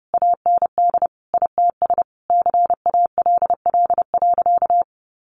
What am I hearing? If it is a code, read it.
Morse code, heard as a keyed tone, about 30 wpm, 700 Hz. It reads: AND ITS CALL.
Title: Sentences from Top 1300 Words 30wpm